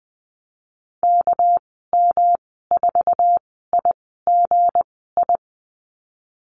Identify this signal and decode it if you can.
Morse code, heard as a keyed tone, about 19 wpm, spaced as for 14 wpm, 700 Hz. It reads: KM4IGI
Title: 219 Lesson 35g - Number 0 - Call Signs 20wpm